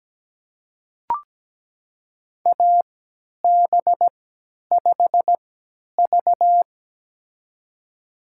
Morse code, heard as a keyed tone, about 17 wpm, spaced as for 10 wpm, 700 Hz. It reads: AB5V
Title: USA Call Signs - 4 Characters - 3x Character Spacing - 17wpm